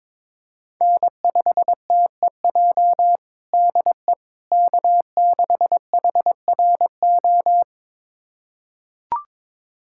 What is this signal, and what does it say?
Morse code, heard as a keyed tone, about 22 wpm, 700 Hz. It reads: N5TEJ DE K6HRO
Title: USA Callsign DE Callsign 22wpm